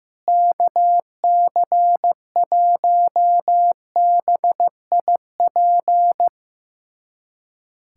Morse code, heard as a keyed tone, about 16 wpm, 700 Hz. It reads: KC1BIP
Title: USA Callsign DE Callsign 15wpm